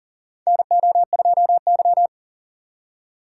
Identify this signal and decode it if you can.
Morse code, heard as a keyed tone, about 40 wpm, 700 Hz. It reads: NO2Y
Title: USA Call Signs - 6x Character Spacing - 40wpm